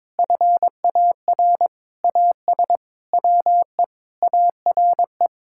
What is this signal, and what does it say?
Morse code, heard as a keyed tone, about 22 wpm, 700 Hz. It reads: FAR AS WE ARE
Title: Sentences from Top 1000 Words - Review Part 6 - 22wpm